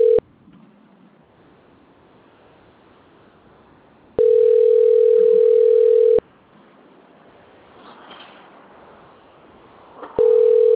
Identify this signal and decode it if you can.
Ringback tone (North America), 440 + 480 Hz.